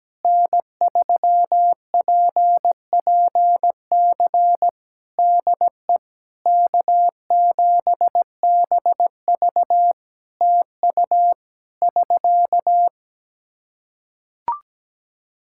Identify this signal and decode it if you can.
Morse code, heard as a keyed tone, about 17 wpm, 700 Hz. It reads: N3PPC DE K7BV TU <SK>
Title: Practice QSO - Exchange 4 - Ending - 17wpm